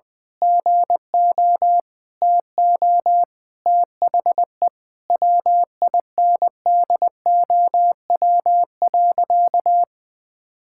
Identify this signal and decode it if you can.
Morse code, heard as a keyed tone, about 20 wpm, 700 Hz. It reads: GO TO THE WINDOW.
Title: Sentences from Top 1600 Words 20wpm